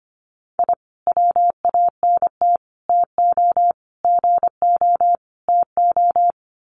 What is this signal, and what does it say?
Morse code, heard as a keyed tone, about 25 wpm, 700 Hz. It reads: I WANT TO GO TO
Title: Sets of 5 Words Encore 25wpm